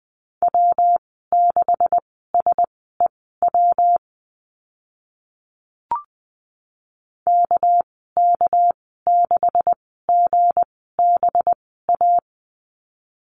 Morse code, heard as a keyed tone, about 20 wpm, spaced as for 14 wpm, 700 Hz. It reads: W6SEW KK6GBA
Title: 219 Lesson 35g - Number 0 - Call Signs 20wpm